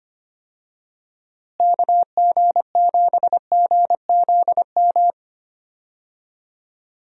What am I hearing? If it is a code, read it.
Morse code, heard as a keyed tone, about 25 wpm, 700 Hz. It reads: KG7GZM
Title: USA Call Signs - 6 Characters - Repeated 2x - 25wpm